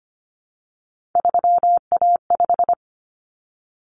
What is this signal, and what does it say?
Morse code, heard as a keyed tone, about 25 wpm, 700 Hz. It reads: 3A5
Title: Common Cut Numbers - 3-Digit 25wpm